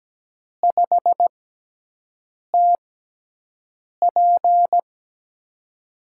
Morse code, heard as a keyed tone, about 17 wpm, spaced as for 6 wpm, 700 Hz. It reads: 5TP
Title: USA Call Signs - 6 Characters - 6x Character Spacing - 17wpm